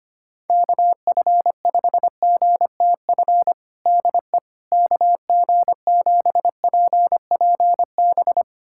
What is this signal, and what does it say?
Morse code, heard as a keyed tone, about 25 wpm, 700 Hz. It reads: KF5GTF DE KG7PPB
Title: USA Callsign DE Callsign 25wpm